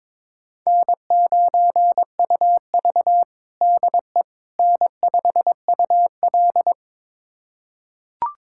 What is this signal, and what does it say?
Morse code, heard as a keyed tone, about 22 wpm, 700 Hz. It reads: N9UV DE N5UL
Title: USA Callsign DE Callsign - 4 Characters - 22wpm